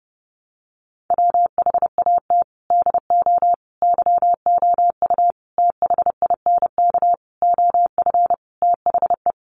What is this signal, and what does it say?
Morse code, heard as a keyed tone, about 30 wpm, 700 Hz. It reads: WHAT DO YOU THINK OF THE
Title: Sentences from Top 300 Words 30wpm